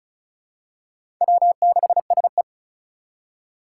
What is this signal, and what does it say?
Morse code, heard as a keyed tone, about 36 wpm, 700 Hz. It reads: W6SE
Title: USA Call Signs - 2x Character Spacing - 35wpm